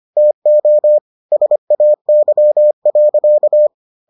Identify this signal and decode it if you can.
Morse code, heard as a keyed tone, about 25 wpm, 600 Hz. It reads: TO SAY.